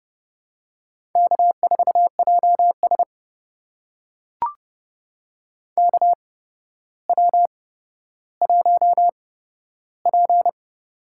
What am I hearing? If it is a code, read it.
Morse code, heard as a keyed tone, about 30 wpm, spaced as for 13 wpm, 700 Hz. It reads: K4JS KW1P
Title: USA Call Signs - 4 Characters - 8x Character Spacing - 30wpm